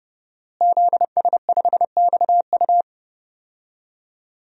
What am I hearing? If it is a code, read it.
Morse code, heard as a keyed tone, about 30 wpm, 700 Hz. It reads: ZS5XU
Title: DXCC-100 Call Signs - 5 Characters - Repeated 3x - 30wpm